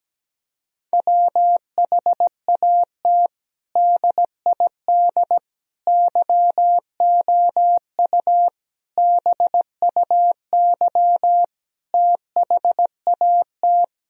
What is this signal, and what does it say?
Morse code, heard as a keyed tone, about 17 wpm, 700 Hz. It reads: WHAT DID YOU BUY THAT
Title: Sentences from Top 1000 Words - Review Part 4 - 17wpm